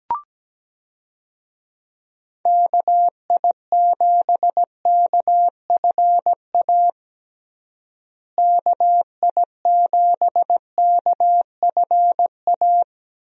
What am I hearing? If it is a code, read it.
Morse code, heard as a keyed tone, about 17 wpm, 700 Hz. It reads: KI7KFA KI7KFA
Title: USA Call Signs - 6 Characters - Repeated 3x - 17wpm